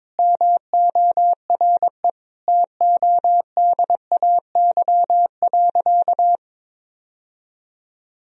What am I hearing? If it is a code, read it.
Morse code, heard as a keyed tone, about 22 wpm, 700 Hz. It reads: MORE TODAY.
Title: Sentences from Top 1000 Words - Review Part 4 - 22wpm